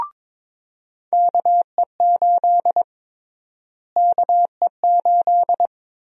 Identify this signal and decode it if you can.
Morse code, heard as a keyed tone, about 22 wpm, 700 Hz. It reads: KE8 KE8
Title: USA Call Sign Prefixes - Repeated 2x - 22wpm